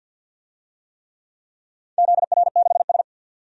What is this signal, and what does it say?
Morse code, heard as a keyed tone, about 52 wpm, 700 Hz. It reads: ZR6S
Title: DXCC-100 Call Signs - 4 Characters - 50wpm